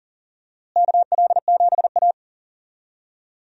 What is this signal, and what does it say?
Morse code, heard as a keyed tone, about 37 wpm, 700 Hz. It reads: KL7A
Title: DXCC-100 Call Signs - 4 Characters - Repeated 3x - 40wpm